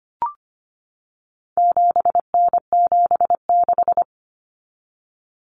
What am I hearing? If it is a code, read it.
Morse code, heard as a keyed tone, about 25 wpm, 700 Hz. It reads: 7N76
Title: Common Cut Numbers - 4-Digit 25wpm